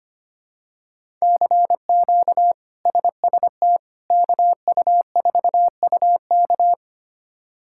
Morse code, heard as a keyed tone, about 25 wpm, 700 Hz. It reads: CQ SST KU4UK
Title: K1USN SST QSO Practice - 25wpm